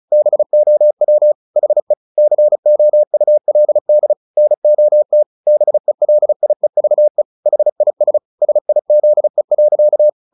Morse code, heard as a keyed tone, about 34 wpm, 600 Hz. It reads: BOW HE COULD NOT BELIEVE HIS SIZE.
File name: the old man and the sea 2 35wpm